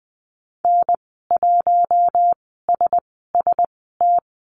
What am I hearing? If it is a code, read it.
Morse code, heard as a keyed tone, about 20 wpm, spaced as for 15 wpm, 700 Hz. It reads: N1SST
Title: 042 Lesson 8g - Number 1 - Call Signs 20wpm